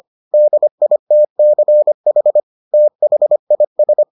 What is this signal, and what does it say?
Morse code, heard as a keyed tone, about 25 wpm, 600 Hz. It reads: DITCH THIS